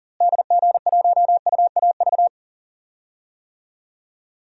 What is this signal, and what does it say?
Morse code, heard as a keyed tone, about 40 wpm, 700 Hz. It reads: DG1UAV
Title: DXCC-100 Call Signs - 6 Characters - Repeated 2x - 40wpm